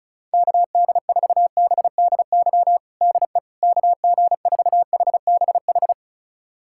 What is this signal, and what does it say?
Morse code, heard as a keyed tone, about 36 wpm, 700 Hz. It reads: KD4BDY DE KG4HBH
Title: USA Callsign DE Callsign 35wpm